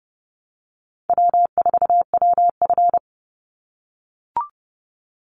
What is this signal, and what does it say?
Morse code, heard as a keyed tone, about 30 wpm, 700 Hz. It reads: W4WF
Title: Call Signs - 30wpm